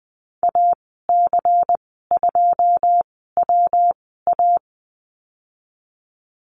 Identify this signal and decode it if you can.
Morse code, heard as a keyed tone, about 20 wpm, spaced as for 15 wpm, 700 Hz. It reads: AC2WA
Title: 140 Lesson 23g - Letter f - Call Signs 20wpm